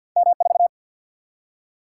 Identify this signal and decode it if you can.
Morse code, heard as a keyed tone, about 50 wpm, 700 Hz. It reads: M4
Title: Single-Letter Single-Number 50wpm